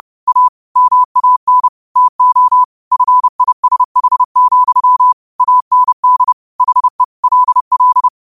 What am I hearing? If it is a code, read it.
Morse code, heard as a keyed tone, about 30 wpm, 1000 Hz. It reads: A MAN TO FISH, AND HE LL
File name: qotdfiles / 30wpm